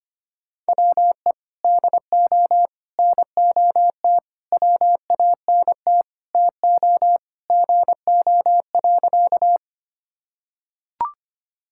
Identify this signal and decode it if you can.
Morse code, heard as a keyed tone, about 25 wpm, 700 Hz. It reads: WE DO NOT WANT TO GO.
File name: practice / 25wpm / Sentences with 4-Word Phrases - 25wpm